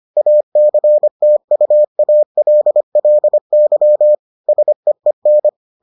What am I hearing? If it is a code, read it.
Morse code, heard as a keyed tone, about 25 wpm, 600 Hz. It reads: ACTUALLY SEEN